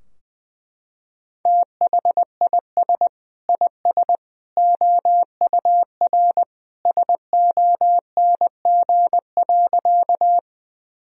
Morse code, heard as a keyed tone, about 20 wpm, 700 Hz. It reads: THIS IS OUR SONG.
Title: Sentences from Top 1000 Words - Review Part 2 - 20wpm